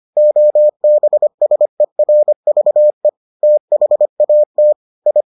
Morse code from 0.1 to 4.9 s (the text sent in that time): OBSERVE THAT